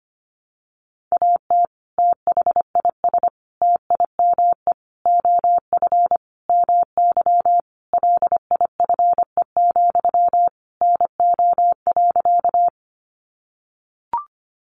Morse code, heard as a keyed tone, about 25 wpm, 700 Hz. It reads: AT THIS TIME OF MY LIFE, NO.
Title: Sentences from Top 300 Words 25wpm